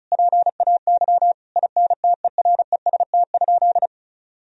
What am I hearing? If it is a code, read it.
Morse code, heard as a keyed tone, about 36 wpm, 700 Hz. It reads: PAY INTEREST?
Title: Sentences from Top 1000 Words - Review Part 2 - 35wpm